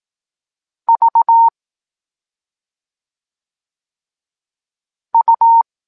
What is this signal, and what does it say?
Morse code, heard as a keyed tone, about 18 wpm, 920 Hz. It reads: V U